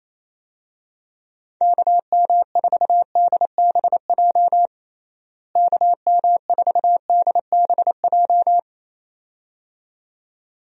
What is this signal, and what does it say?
Morse code, heard as a keyed tone, about 28 wpm, 700 Hz. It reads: KM4DBJ KM4DBJ
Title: USA Call Signs - 6 Characters - Repeated 2x - 28wpm